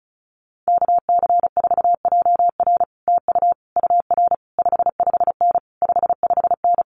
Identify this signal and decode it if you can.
Morse code, heard as a keyed tone, about 36 wpm, 700 Hz. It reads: KC4JR TU UR 55N 55N
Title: Common QSO Phrases Encore 35wpm